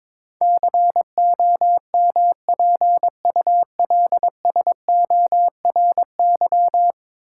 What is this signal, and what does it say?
Morse code, heard as a keyed tone, about 21 wpm, 700 Hz. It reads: COMPULSORY
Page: morse-code-ninja-practice-22wpm